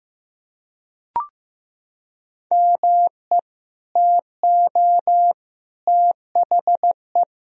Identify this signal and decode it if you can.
Morse code, heard as a keyed tone, about 15 wpm, 700 Hz. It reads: ME TO THE
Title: Sentences from Top 400 Words and Abbreviations 15wpm